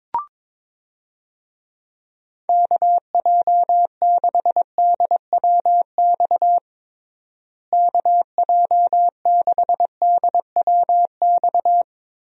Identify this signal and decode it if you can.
Morse code, heard as a keyed tone, about 22 wpm, 700 Hz. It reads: KJ6DWX KJ6DWX
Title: USA Call Signs - 6 Characters - Repeated 3x - 22wpm